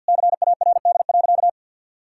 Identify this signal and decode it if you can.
Morse code, heard as a keyed tone, about 52 wpm, 700 Hz. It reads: CARD.
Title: Sentences from Top 1000 Words - Review Part 6 - 50wpm